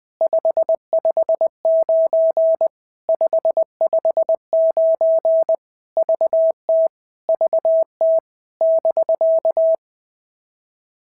Morse code, heard as a keyed tone, about 20 wpm, 650 Hz. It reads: 559 559 VT VT <BK>
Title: POTA QSO Practice - 20wpm